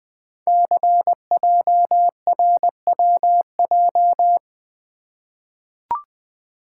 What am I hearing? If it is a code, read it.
Morse code, heard as a keyed tone, about 20 wpm, 700 Hz. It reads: CJRWJ
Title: Code Groups - 5 Letters - 20wpm